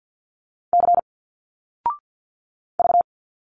Morse code, heard as a keyed tone, about 50 wpm, 700 Hz. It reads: C V
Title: Single Letters 50wpm